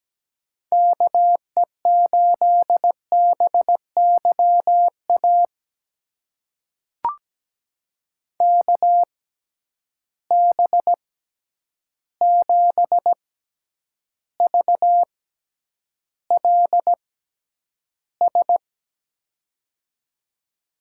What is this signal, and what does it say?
Morse code, heard as a keyed tone, about 17 wpm, spaced as for 9 wpm, 700 Hz. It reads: KE8BYA KB7VLS
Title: USA Call Signs - 6 Characters - 6x Character Spacing - 17wpm